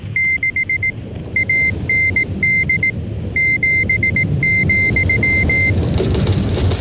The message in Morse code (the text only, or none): 6 AND 7,